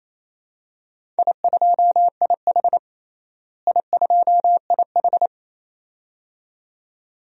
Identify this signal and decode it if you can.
Morse code, heard as a keyed tone, about 28 wpm, 700 Hz. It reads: I2IH I2IH
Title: DXCC-100 Call Signs - 4 Characters - Repeated 2x - 28wpm